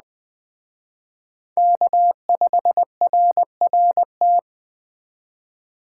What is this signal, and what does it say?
Morse code, heard as a keyed tone, about 20 wpm, 700 Hz. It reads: K5RRT
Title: Speed Racing - USA 1x3 Call Signs 17wpm